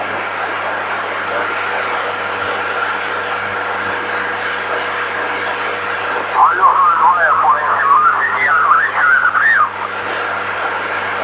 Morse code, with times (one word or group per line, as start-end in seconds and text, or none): none